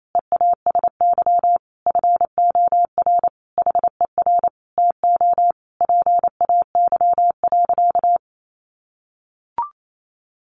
Morse code, heard as a keyed tone, about 28 wpm, 700 Hz. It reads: EASY FOR HER TO PAY.
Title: Sentences from Top 1500 Words 28wpm